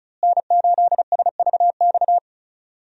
Morse code, heard as a keyed tone, about 34 wpm, 700 Hz. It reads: N8SVX
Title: USA Call Signs - 5 Characters - Repeated 2x - 35wpm